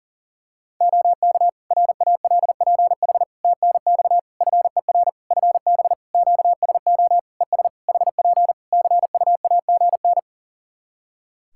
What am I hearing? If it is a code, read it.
Morse code, heard as a keyed tone, about 38 wpm, 700 Hz. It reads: OK RALPH TNX FER FB QSO ES HP CUAGN
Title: Practice QSO - Exchange 4 - Ending - 40wpm